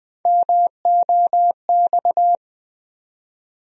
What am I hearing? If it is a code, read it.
Morse code, heard as a keyed tone, about 21 wpm, 700 Hz. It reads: MOX
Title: USA Call Sign Postfixes - 3 Letters 20wpm